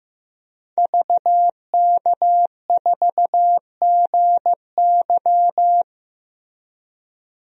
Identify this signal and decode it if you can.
Morse code, heard as a keyed tone, about 15 wpm, 700 Hz. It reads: VK4GY
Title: DXCC-100 Call Signs - 5 Characters - Repeated 2x - 15wpm